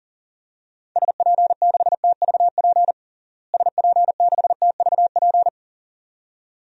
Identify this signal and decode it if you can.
Morse code, heard as a keyed tone, about 37 wpm, 700 Hz. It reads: SP6TVP SP6TVP
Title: DXCC-100 Call Signs - 6 Characters - Repeated 2x - 40wpm